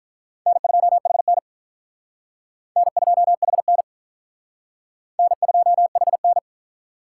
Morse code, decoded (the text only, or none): N2HN N2HN N2HN